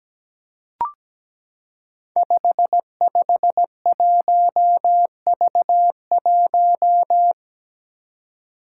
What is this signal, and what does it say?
Morse code, heard as a keyed tone, about 17 wpm, 700 Hz. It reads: 551V1
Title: Five Challenging Letter-Number Combinations 17wpm